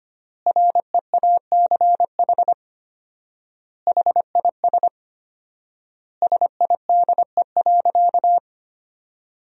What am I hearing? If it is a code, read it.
Morse code, heard as a keyed tone, about 25 wpm, 700 Hz. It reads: REACH HIS SIDE.
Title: 4x Word-Spacing - Sentences from Top 400 Words - 25wpm